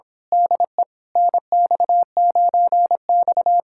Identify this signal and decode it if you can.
Morse code, heard as a keyed tone, about 26 wpm, 700 Hz. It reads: DE NX9X
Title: Speed-Racing Common QSO Phrases - 22wpm